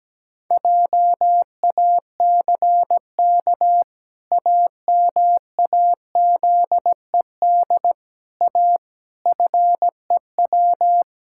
Morse code, heard as a keyed tone, about 17 wpm, 700 Hz. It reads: JACK AMAZED A FEW
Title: Pangram Challenge v2 - 17wpm